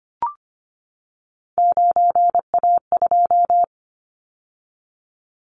Morse code, heard as a keyed tone, about 25 wpm, 700 Hz. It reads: 9A2
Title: Callsign Prefixes for Top 50 DXCC - 25wpm